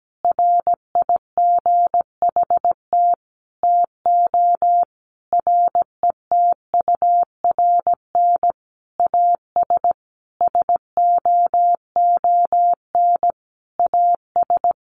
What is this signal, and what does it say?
Morse code, heard as a keyed tone, about 17 wpm, 700 Hz. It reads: RIGHT TO RETURN AS SOON AS
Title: Sentences from Top 800 Words 17wpm